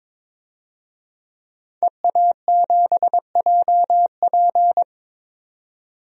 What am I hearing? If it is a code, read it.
Morse code, heard as a keyed tone, about 22 wpm, 700 Hz. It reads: EA7JP
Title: DXCC-100 Call Signs - 5 Characters - 22wpm